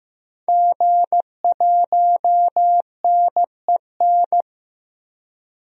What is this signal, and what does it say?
Morse code, heard as a keyed tone, about 15 wpm, 700 Hz. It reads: G1NEN
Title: DXCC-100 Call Signs - 5 Characters - 15wpm